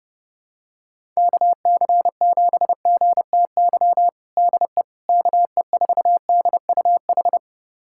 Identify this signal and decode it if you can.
Morse code, heard as a keyed tone, about 30 wpm, 700 Hz. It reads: KC7GTY DE KE4DUH
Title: USA Callsign DE Callsign - 6 Characters - 30wpm